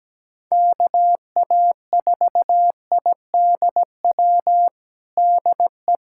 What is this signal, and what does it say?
Morse code, heard as a keyed tone, about 17 wpm, 700 Hz. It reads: KA4IDW DE
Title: USA Callsign DE Callsign 17wpm